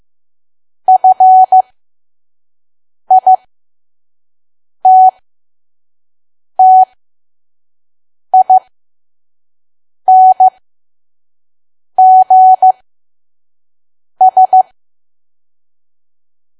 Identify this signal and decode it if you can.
Morse code, heard as a keyed tone, about 15 wpm, spaced as for 5 wpm, 750 Hz. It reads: FITTINGS
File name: Archive / 5 WPM files / PM